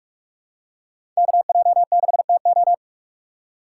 Morse code, heard as a keyed tone, about 46 wpm, 700 Hz. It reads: KJ6TO
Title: USA Call Signs - 5 Characters - 8x Character Spacing - 45wpm